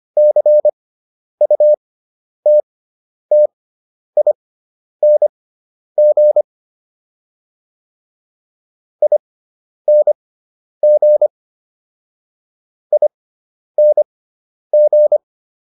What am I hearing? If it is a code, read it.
Morse code, heard as a keyed tone, about 25 wpm, spaced as for 10 wpm, 600 Hz. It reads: CUTTING ING ING